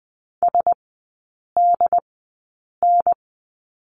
Morse code, heard as a keyed tone, about 20 wpm, spaced as for 8 wpm, 700 Hz. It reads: SDN
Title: 087 Lesson 15e - Number 5 - Three Characters 20wpm